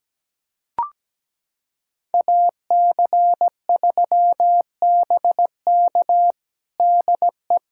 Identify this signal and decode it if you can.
Morse code, heard as a keyed tone, about 17 wpm, 700 Hz. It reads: AC3BK DE
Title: USA Callsign DE Callsign - 5 Characters - 17wpm